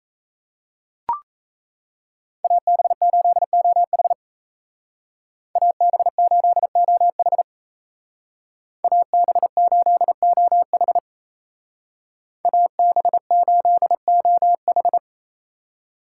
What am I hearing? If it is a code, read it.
Morse code, heard as a keyed tone, about 34 wpm, 700 Hz. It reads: AB8OH AB8OH AB8OH AB8OH
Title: Speed Racing - USA 2x2 Call Signs 28wpm